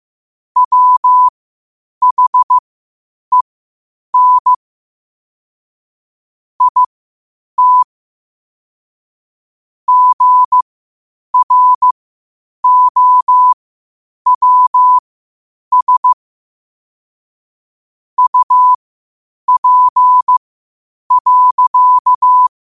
Morse code, heard as a keyed tone, about 15 wpm, spaced as for 8 wpm, 1000 Hz. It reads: WHEN IT GROWS UP.
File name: qotdfiles / 8wpm